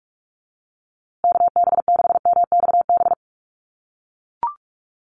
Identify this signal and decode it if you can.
Morse code, heard as a keyed tone, about 44 wpm, 700 Hz. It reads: KB6MXB
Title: USA Call Signs Repeated 2x 45wpm